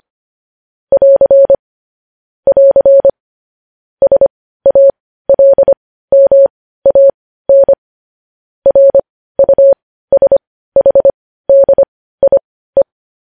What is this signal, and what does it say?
Morse code, heard as a keyed tone, about 25 wpm, spaced as for 15 wpm, 550 Hz. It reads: <AR> <AR> SALMAN RUSHDIE